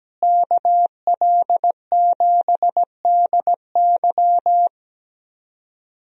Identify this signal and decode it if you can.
Morse code, heard as a keyed tone, about 17 wpm, 700 Hz. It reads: KL7DY
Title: DXCC-100 Call Signs - 5 Characters - Repeated 2x - 17wpm